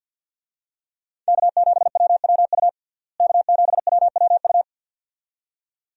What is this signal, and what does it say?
Morse code, heard as a keyed tone, about 46 wpm, 700 Hz. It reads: K7WWU K7WWU
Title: USA Call Signs - 5 Characters - Repeated 2x - 50wpm